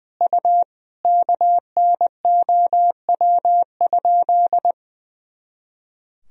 Morse code, heard as a keyed tone, about 20 wpm, 700 Hz. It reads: U KNOW?
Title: Sentences from Top 100 Words and Abbreviations 20wpm